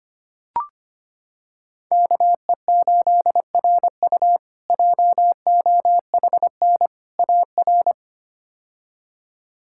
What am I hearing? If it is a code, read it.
Morse code, heard as a keyed tone, about 25 wpm, 700 Hz. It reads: KE8RU JOHN AR
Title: K1USN SST QSO Practice - 25wpm